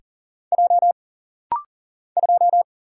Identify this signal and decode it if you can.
Morse code, heard as a keyed tone, about 40 wpm, 700 Hz. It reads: J 2
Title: Single Letter-Number Alt - Rapid-Fire 40wpm